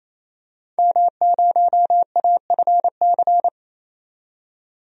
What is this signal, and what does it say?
Morse code, heard as a keyed tone, about 28 wpm, 700 Hz. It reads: M0AFC
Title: DXCC-100 Call Signs - 5 Characters - 28wpm